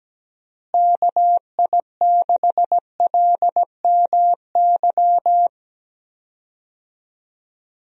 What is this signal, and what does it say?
Morse code, heard as a keyed tone, about 17 wpm, 700 Hz. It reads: KI6LMY
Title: USA Call Signs - 6 Characters - Repeated 3x - 17wpm